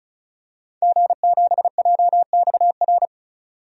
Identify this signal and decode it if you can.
Morse code, heard as a keyed tone, about 35 wpm, 700 Hz. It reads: G7JXR
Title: DXCC-100 Call Signs - 5 Characters - 35wpm